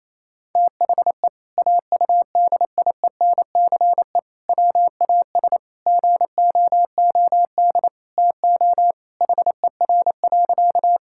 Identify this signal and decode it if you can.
Morse code, heard as a keyed tone, about 28 wpm, 700 Hz. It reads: THE AUDIENCE WAS GOOD TO HER.
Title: Sentences from Top 1000 Words - Review Part 5 - 28wpm